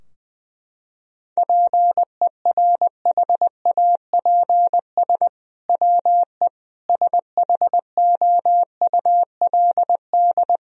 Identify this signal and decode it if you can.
Morse code, heard as a keyed tone, about 20 wpm, 700 Hz. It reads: PERHAPS WE SHOULD
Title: Sentences from Top 1000 Words - Review Part 3 - 20wpm